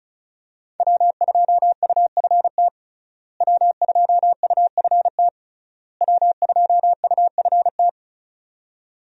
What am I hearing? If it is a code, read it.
Morse code, heard as a keyed tone, about 36 wpm, 700 Hz. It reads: W2UFT W2UFT W2UFT
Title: USA Call Signs - 5 Characters - Repeated 3x - 35wpm